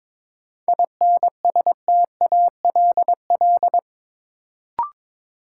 Morse code, heard as a keyed tone, about 22 wpm, 700 Hz. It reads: INSTALL
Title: Sentences from Top 1600 Words - New Word Drill Alt - 22wpm